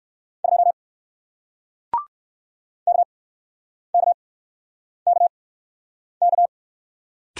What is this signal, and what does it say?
Morse code, heard as a keyed tone, about 60 wpm, spaced as for 9 wpm, 700 Hz. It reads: 2 KKKK